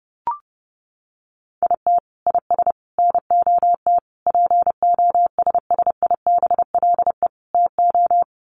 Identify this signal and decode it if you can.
Morse code, heard as a keyed tone, about 30 wpm, 700 Hz. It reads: IT IS NOT POSSIBLE TO
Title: Sets of 5 Words Encore2 30wpm